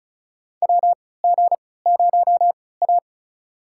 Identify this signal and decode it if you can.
Morse code, heard as a keyed tone, about 36 wpm, spaced as for 20 wpm, 700 Hz. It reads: WG0A
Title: USA Call Signs - 4 Characters - 3x Character Spacing - 35wpm